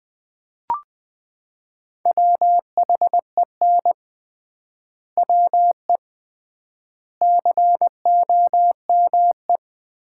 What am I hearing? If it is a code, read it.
Morse code, heard as a keyed tone, about 21 wpm, 700 Hz. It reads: WHEN WE COME
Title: 3x Word-Spacing - Sets of 4 Words - 20wpm